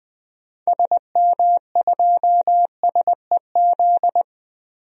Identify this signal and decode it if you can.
Morse code, heard as a keyed tone, about 20 wpm, 700 Hz. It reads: SM2SEZ
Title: DXCC-100 Call Signs - 6 Characters - 20wpm